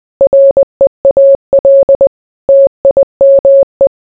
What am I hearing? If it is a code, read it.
Morse code, heard as a keyed tone, about 20 wpm, 550 Hz. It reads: REAL TIME